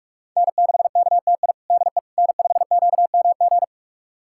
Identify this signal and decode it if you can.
Morse code, heard as a keyed tone, about 43 wpm, 700 Hz. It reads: N6KTI DE N5QMG
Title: USA Callsign DE Callsign - 5 Characters - 45wpm